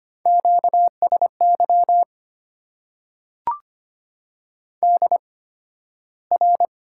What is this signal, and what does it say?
Morse code, heard as a keyed tone, about 25 wpm, spaced as for 13 wpm, 700 Hz. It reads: QSY DR
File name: Top 90 QSO Elements - 8x Character Spacing - 25wpm